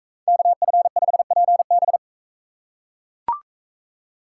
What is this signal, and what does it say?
Morse code, heard as a keyed tone, about 42 wpm, 700 Hz. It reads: KF5PB